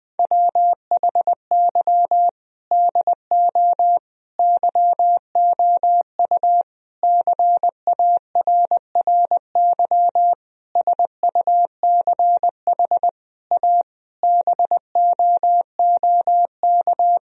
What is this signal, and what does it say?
Morse code, heard as a keyed tone, about 20 wpm, 700 Hz. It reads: WHY DO YOU CARRY SUCH A BOOK
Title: Sentences from Top 1000 Words - Review Part 1 - 20wpm